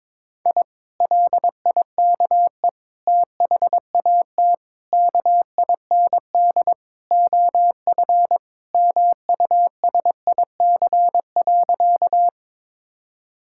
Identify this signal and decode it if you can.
Morse code, heard as a keyed tone, about 22 wpm, 700 Hz. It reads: I LIKE THAT KIND OF MUSIC.
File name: practice / 22wpm / Sentences with 3-Word Phrases - 22wpm